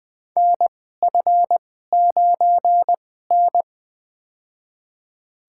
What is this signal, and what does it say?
Morse code, heard as a keyed tone, about 20 wpm, spaced as for 14 wpm, 700 Hz. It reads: NF9N
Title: USA Call Signs - 2x Character Spacing - 20wpm